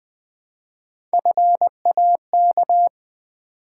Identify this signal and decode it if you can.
Morse code, heard as a keyed tone, about 20 wpm, 700 Hz. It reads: FAK
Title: Build A Word - 1 Letter At A Time - 4-Letter Words - 20wpm